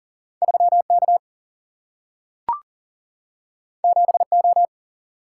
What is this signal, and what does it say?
Morse code, heard as a keyed tone, about 39 wpm, 700 Hz. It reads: 3 K 7 O
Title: Single-Letter Single-Number 40wpm